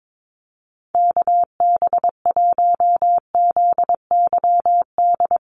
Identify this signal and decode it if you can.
Morse code, heard as a keyed tone, about 22 wpm, 700 Hz. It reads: KB1ZYD
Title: USA Call Signs Repeated 3x 22wpm